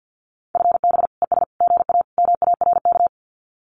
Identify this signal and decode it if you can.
Morse code, heard as a keyed tone, about 50 wpm, 700 Hz. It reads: FB ES GA MARK
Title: Common QSO Phrases Encore 50wpm